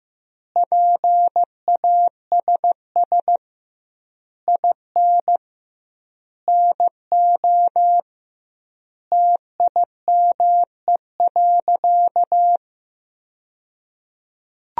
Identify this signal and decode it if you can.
Morse code, heard as a keyed tone, about 15 wpm, 700 Hz. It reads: PASS IN NO TIME.
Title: 2x Word-Spacing - Sentences from Top 400 Words - 15wpm